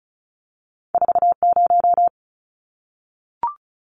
Morse code, heard as a keyed tone, about 36 wpm, 700 Hz. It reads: 40
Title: Numbers - 2-Digit 35wpm